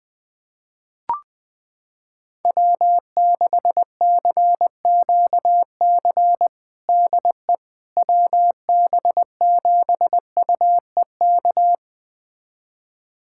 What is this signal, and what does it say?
Morse code, heard as a keyed tone, about 20 wpm, 700 Hz. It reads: W6CQC DE WB7UEK
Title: USA Callsign DE Callsign 20wpm